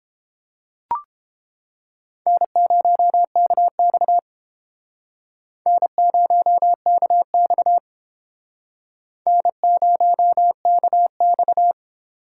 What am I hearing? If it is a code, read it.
Morse code, heard as a keyed tone, about 29 wpm, 700 Hz. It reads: N0KX N0KX N0KX